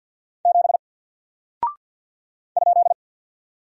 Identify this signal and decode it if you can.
Morse code, heard as a keyed tone, about 48 wpm, 700 Hz. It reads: 7 ?
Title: Single Character (Letter:Number:Punctuation:Prosign) Alt 50wpm